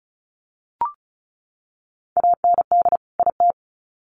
Morse code, heard as a keyed tone, about 34 wpm, 700 Hz. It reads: AND IT
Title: Sets of 2 Words 35wpm